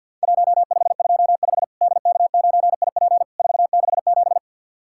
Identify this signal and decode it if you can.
Morse code, heard as a keyed tone, about 50 wpm, 700 Hz. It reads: 1525 DK9IP 467
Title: Worked All Europe DX Contest - QTC Exchange - 50wpm